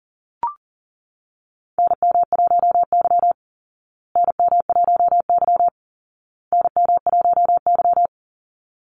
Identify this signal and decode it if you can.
Morse code, heard as a keyed tone, about 40 wpm, 700 Hz. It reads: NM1Y NM1Y NM1Y
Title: Call Signs Repeated 2x - 40wpm